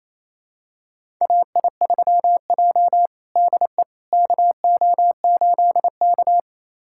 Morse code, heard as a keyed tone, about 28 wpm, 700 Hz. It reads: AI3J DE KO8K
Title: USA Callsign DE Callsign - 4 Characters - 28wpm